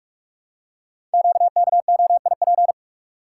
Morse code, heard as a keyed tone, about 44 wpm, 700 Hz. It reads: QKOIP
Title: Code Groups - 5 Letters - 45wpm